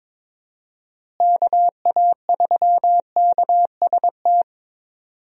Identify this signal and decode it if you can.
Morse code, heard as a keyed tone, about 22 wpm, 700 Hz. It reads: KA3KST
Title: USA Call Signs - 6 Characters - Repeated 3x - 22wpm